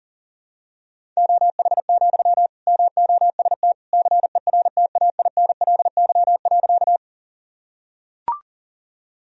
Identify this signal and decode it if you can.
Morse code, heard as a keyed tone, about 40 wpm, 700 Hz. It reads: OH, MOST CERTAINLY.
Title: Sentences from Top 1000 Words - Review Part 1 - 40wpm